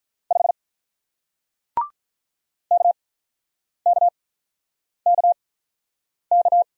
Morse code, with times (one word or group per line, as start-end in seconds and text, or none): 0.3-0.5 5
2.7-6.6 KKKK